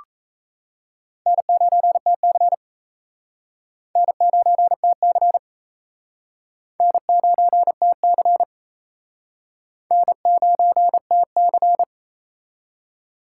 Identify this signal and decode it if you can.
Morse code, heard as a keyed tone, about 35 wpm, 700 Hz. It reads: N9TC N9TC N9TC N9TC
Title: Speed Racing - USA 1x2 Call Signs 28wpm